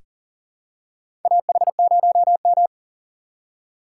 Morse code, heard as a keyed tone, about 37 wpm, 700 Hz. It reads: AH0M
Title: USA Call Signs - 4 Characters - 8x Character Spacing - 40wpm